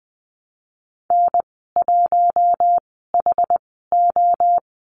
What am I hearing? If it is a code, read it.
Morse code, heard as a keyed tone, about 20 wpm, spaced as for 15 wpm, 700 Hz. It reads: N1HO
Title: 062 Lesson 11g - Letter h - Call Signs 20wpm